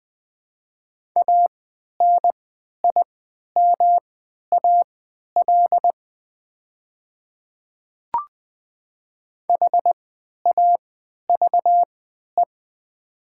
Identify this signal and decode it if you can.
Morse code, heard as a keyed tone, about 21 wpm, spaced as for 12 wpm, 700 Hz. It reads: ANIMAL HAVE
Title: 3x Character Spacing - Top 300 Words - 20wpm